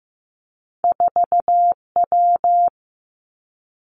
Morse code, heard as a keyed tone, about 15 wpm, 700 Hz. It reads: 4W
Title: Two Challenging Letter-Number Combinations 15wpm